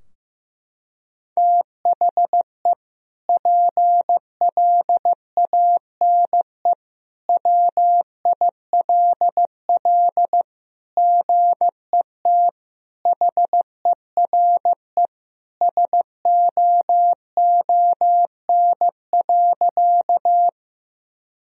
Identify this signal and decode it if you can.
Morse code, heard as a keyed tone, about 15 wpm, 700 Hz. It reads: THE PLANE WILL GET HERE SOON.
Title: Sentences from Top 1000 Words - Review Part 3 - 15wpm